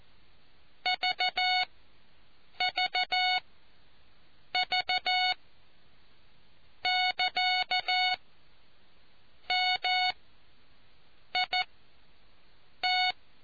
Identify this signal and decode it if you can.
Morse code, heard as a keyed tone, about 14 wpm, spaced as for 6 wpm, 740 Hz. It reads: VVV<KA>MIT